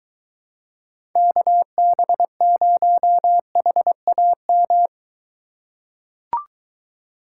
Morse code, heard as a keyed tone, about 23 wpm, 700 Hz. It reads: KB0HAM